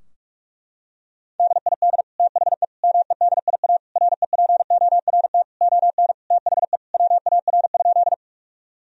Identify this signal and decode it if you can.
Morse code, heard as a keyed tone, about 46 wpm, 700 Hz. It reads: DID THE MEDIA REPORT ON THE WAR?
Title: Sentences from Top 1000 Words - Review Part 4 - 45wpm